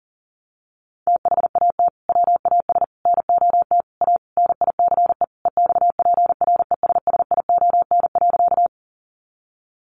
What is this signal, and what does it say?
Morse code, heard as a keyed tone, about 40 wpm, 700 Hz. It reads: THAT WAS NOT A NICE EXPRESSION.
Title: Sentences from Top 1200 Words 40wpm